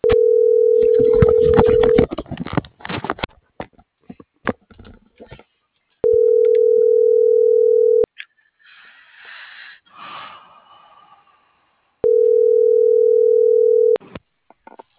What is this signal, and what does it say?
Ringback tone (North America), 440 + 480 Hz.